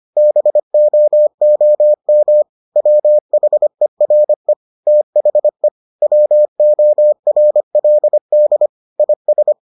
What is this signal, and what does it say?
Morse code, heard as a keyed tone, about 25 wpm, 600 Hz. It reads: BOOM WHERE THE WORLD IS